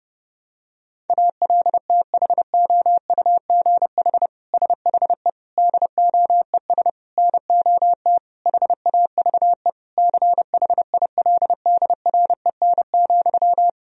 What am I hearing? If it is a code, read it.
Morse code, heard as a keyed tone, about 30 wpm, 700 Hz. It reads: ALTHOUGH SHE DOES NOT HAVE CHILDREN,
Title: Sentences from Top 1000 Words - Review Part 3 - 30wpm